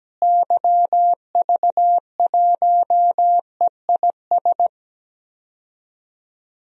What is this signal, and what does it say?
Morse code, heard as a keyed tone, about 17 wpm, 700 Hz. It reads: YV1EIS
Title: DXCC-100 Call Signs - 6 Characters - Repeated 3x - 17wpm